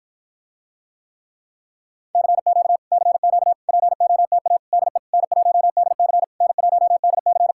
Morse code, heard as a keyed tone, about 60 wpm, 700 Hz. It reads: CQ CQ POTA DE N1DC N1DC